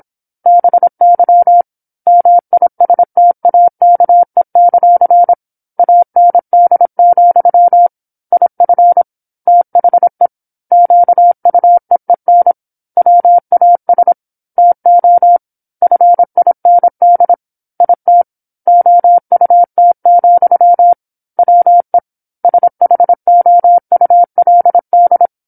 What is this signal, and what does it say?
Morse code, heard as a keyed tone, about 27 wpm, 700 Hz. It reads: BY MISTAKE; AND, IF THE QUEEN WAS TO FIND IT OUT, WE SHOULD
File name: 08-Chapter-08 26 WPM